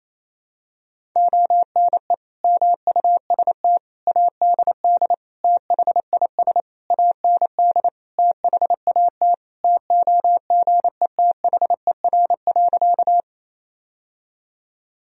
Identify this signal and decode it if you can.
Morse code, heard as a keyed tone, about 28 wpm, 700 Hz. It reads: ONE MUST ADD THIS AND THAT TOGETHER.
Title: Sentences from Top 1000 Words - Review Part 1 - 28wpm